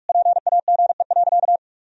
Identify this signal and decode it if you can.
Morse code, heard as a keyed tone, about 46 wpm, 700 Hz. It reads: PAGE.
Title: Sentences from Top 1000 Words - Review Part 1 - 45wpm